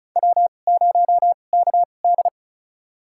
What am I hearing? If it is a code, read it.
Morse code, heard as a keyed tone, about 34 wpm, spaced as for 25 wpm, 700 Hz. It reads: W0KD
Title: USA Call Signs - 4 Characters - 2x Character Spacing - 35wpm